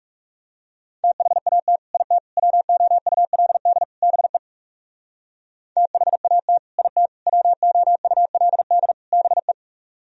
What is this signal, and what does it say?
Morse code, heard as a keyed tone, about 38 wpm, 700 Hz. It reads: THAT IT WOULD BE THAT IT WOULD BE